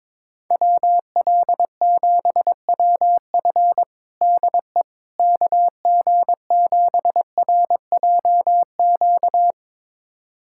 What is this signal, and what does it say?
Morse code, heard as a keyed tone, about 21 wpm, 700 Hz. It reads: WL7WF DE KG7RJQ
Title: USA Callsign DE Callsign 22wpm